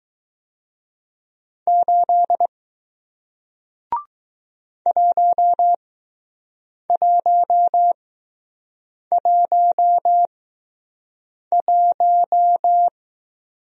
Morse code, heard as a keyed tone, about 20 wpm, spaced as for 6 wpm, 700 Hz. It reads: 8 1111